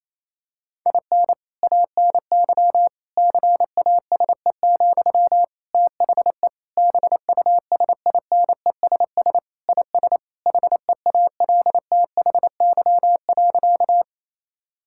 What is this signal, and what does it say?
Morse code, heard as a keyed tone, about 28 wpm, 700 Hz. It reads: IN ANY CASE, THE BUSINESS IS HEALTHY.
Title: Sentences from Top 1000 Words - Review Part 5 - 28wpm